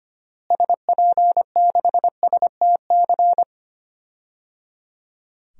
Morse code, heard as a keyed tone, about 25 wpm, 700 Hz. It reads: SP6STC
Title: DXCC-100 Call Signs - 6 Characters - Repeated 3x - 25wpm